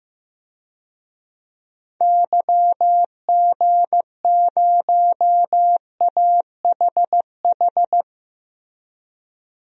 Morse code, heard as a keyed tone, about 15 wpm, 700 Hz. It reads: YG0AHH
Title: DXCC-100 Call Signs - 6 Characters - Repeated 3x - 15wpm